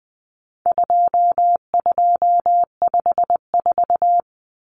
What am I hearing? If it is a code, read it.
Morse code, heard as a keyed tone, about 20 wpm, 700 Hz. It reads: 2254
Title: Common Cut Numbers - 4-Digit 20wpm